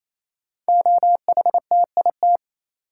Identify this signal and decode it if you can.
Morse code, heard as a keyed tone, about 29 wpm, 700 Hz. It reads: OHTIT
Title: Code Groups - 5 Letters - 28wpm